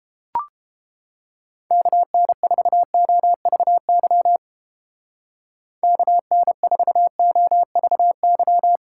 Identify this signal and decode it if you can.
Morse code, heard as a keyed tone, about 28 wpm, 700 Hz. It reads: KN4OVY KN4OVY